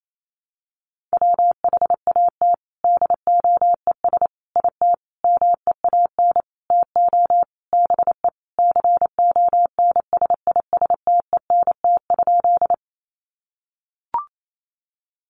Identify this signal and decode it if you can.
Morse code, heard as a keyed tone, about 28 wpm, 700 Hz. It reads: WHAT DOES IT MEAN TO BE CONSISTENT?
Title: Sentences from Top 1500 Words 28wpm